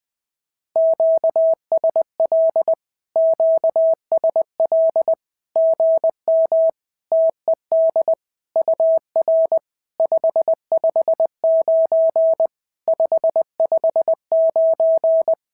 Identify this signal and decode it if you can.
Morse code, heard as a keyed tone, about 20 wpm, 650 Hz. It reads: QSL QSL GM TED UR 559 559
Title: SOTA QSO Practice 20wpm